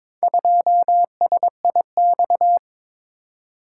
Morse code, heard as a keyed tone, about 21 wpm, 700 Hz. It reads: 2SIX
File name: practice / 22wpm / Four Letter-Number Combinations - 22wpm